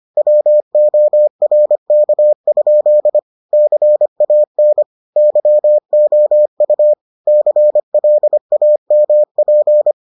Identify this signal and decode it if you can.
Morse code, heard as a keyed tone, about 25 wpm, 600 Hz. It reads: WORK? CAN YOU CLAMP